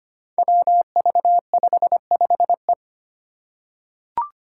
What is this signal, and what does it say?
Morse code, heard as a keyed tone, about 25 wpm, 700 Hz. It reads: WV55E
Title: Five Challenging Letter-Number Combinations 25wpm